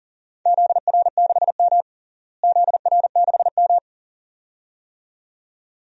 Morse code, heard as a keyed tone, about 40 wpm, 700 Hz. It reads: ZR6M ZR6M
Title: DXCC-100 Call Signs - 4 Characters - Repeated 2x - 40wpm